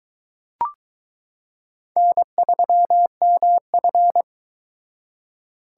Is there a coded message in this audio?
Yes: Morse code from 2.0 to 4.2 s, heard as a keyed tone, about 21 wpm, 700 Hz. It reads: N3MF